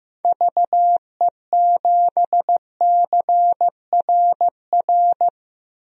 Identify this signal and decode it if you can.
Morse code, heard as a keyed tone, about 15 wpm, 700 Hz. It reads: VE7CRR
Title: DXCC-100 Call Signs - 6 Characters - 15wpm